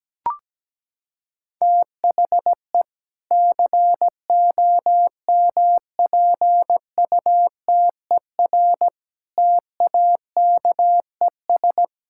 Morse code, heard as a keyed tone, about 17 wpm, 700 Hz. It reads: THE COMPUTER TAKES
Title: Sentences from Top 1000 Words - Review Part 6 - 17wpm